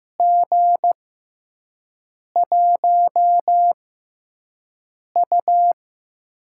Morse code, heard as a keyed tone, about 15 wpm, spaced as for 5 wpm, 700 Hz. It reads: G1U
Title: USA Call Signs - 4 Characters - 6x Character Spacing - 15wpm